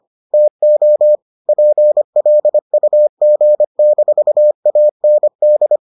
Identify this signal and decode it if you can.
Morse code, heard as a keyed tone, about 25 wpm, 600 Hz. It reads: TO PLUG-AND